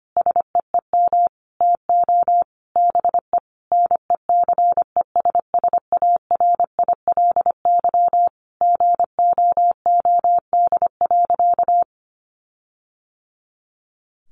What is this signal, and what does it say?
Morse code, heard as a keyed tone, about 25 wpm, 700 Hz. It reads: SEEM TO BE NECESSARILY GOOD.
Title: Sentences from Top 1400 Words 25wpm